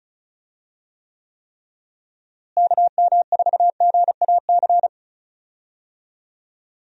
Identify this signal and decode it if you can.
Morse code, heard as a keyed tone, about 36 wpm, 700 Hz. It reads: KM4GAC
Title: USA Call Signs - 6 Characters - Repeated 3x - 35wpm